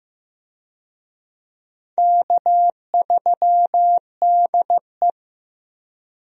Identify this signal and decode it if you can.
Morse code, heard as a keyed tone, about 15 wpm, 700 Hz. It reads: K3DE
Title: USA Call Signs - 4 Characters - Repeated 2x - 15wpm